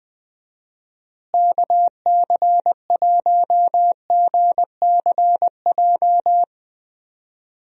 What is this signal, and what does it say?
Morse code, heard as a keyed tone, about 20 wpm, 700 Hz. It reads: KC1GCJ
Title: USA Call Signs - 6 Characters - Repeated 3x - 20wpm